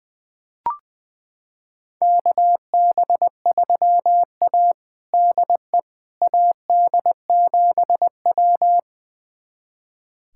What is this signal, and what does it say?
Morse code, heard as a keyed tone, about 20 wpm, 700 Hz. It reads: KB3A DE AD7W
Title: USA Callsign DE Callsign - 4 Characters - 20wpm